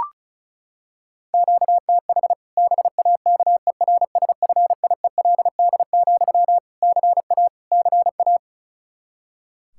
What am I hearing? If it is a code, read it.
Morse code, heard as a keyed tone, about 36 wpm, 700 Hz. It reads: QTH BAKERSFIELD, CA CA
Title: Practice QSO - Exchange 1 - Beginning - 35wpm